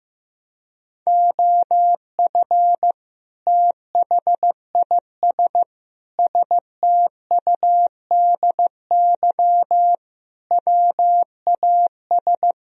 Morse code, heard as a keyed tone, about 16 wpm, 700 Hz. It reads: OF THIS STUDY WAS
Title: Speed-Racing - Sets of 5 Words 15wpm